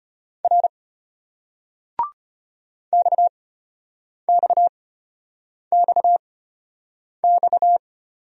Morse code, heard as a keyed tone, about 33 wpm, spaced as for 8 wpm, 700 Hz. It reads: R XXXX